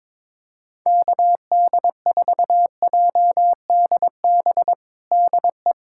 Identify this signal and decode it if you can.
Morse code, heard as a keyed tone, about 22 wpm, 700 Hz. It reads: KD4JDB DE
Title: USA Callsign DE Callsign 22wpm